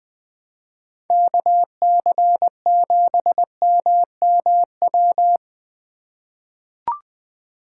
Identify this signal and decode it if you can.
Morse code, heard as a keyed tone, about 21 wpm, 700 Hz. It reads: KC7MMW
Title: USA Call Signs - 4x Character Spacing - 20wpm